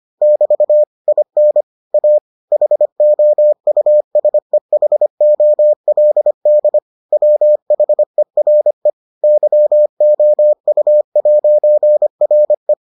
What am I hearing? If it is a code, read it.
Morse code, heard as a keyed tone, about 25 wpm, 600 Hz. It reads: <BT> IN A HOUSEHOLD WHERE YOU'RE